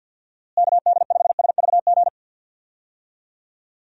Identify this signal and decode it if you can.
Morse code, heard as a keyed tone, about 49 wpm, 700 Hz. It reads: KD5SVG
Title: USA Call Signs - 6 Characters - Repeated 2x - 50wpm